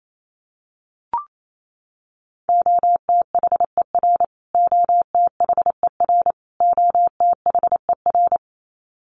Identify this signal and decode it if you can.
Morse code, heard as a keyed tone, about 28 wpm, 700 Hz. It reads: OTHER OTHER OTHER
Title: Top 300 Words Repeated 28wpm